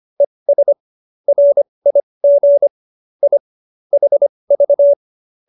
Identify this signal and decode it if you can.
Morse code, heard as a keyed tone, about 25 wpm, spaced as for 20 wpm, 570 Hz. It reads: ES RIG I HV